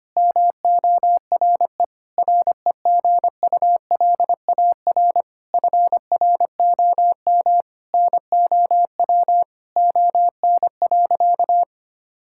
Morse code, heard as a keyed tone, about 25 wpm, 700 Hz. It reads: MORE REGULAR FROM NOW ON.
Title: Sentences from Top 1000 Words - Review Part 6 - 25wpm